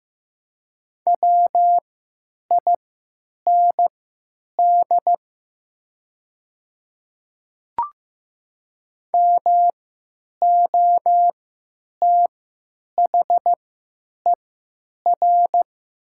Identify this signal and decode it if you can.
Morse code, heard as a keyed tone, about 15 wpm, spaced as for 9 wpm, 700 Hz. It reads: WIND MOTHER
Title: 3x Character Spacing - Top 500 Words - 15wpm